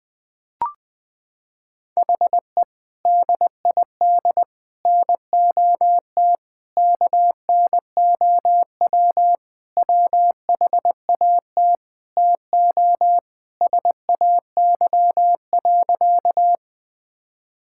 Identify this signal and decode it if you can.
Morse code, heard as a keyed tone, about 20 wpm, 700 Hz. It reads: HE DID NOT KNOW WHAT TO SAY.
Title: Sentences with 5-Word Phrases - 20wpm